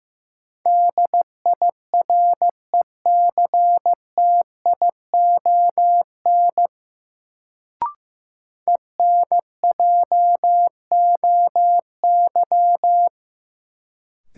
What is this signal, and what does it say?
Morse code, heard as a keyed tone, about 15 wpm, 700 Hz. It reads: DIRECTION ENJOY
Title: Sentences from Top 1300 Words - New Word Drill Alt - 15wpm